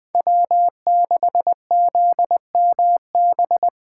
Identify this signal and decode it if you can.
Morse code, heard as a keyed tone, about 23 wpm, 700 Hz. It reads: W6ZMB
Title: Speed Racing - USA 1x3 Call Signs 20wpm